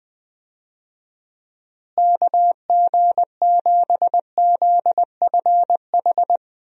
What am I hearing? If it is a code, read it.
Morse code, heard as a keyed tone, about 20 wpm, 700 Hz. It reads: KG7ZFH
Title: USA Call Signs - 6 Characters - Repeated 2x - 20wpm